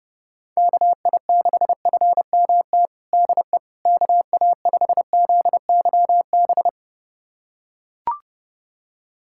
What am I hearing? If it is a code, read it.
Morse code, heard as a keyed tone, about 30 wpm, 700 Hz. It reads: KI6FMT DE KA5ZYB
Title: USA Callsign DE Callsign - 6 Characters - 30wpm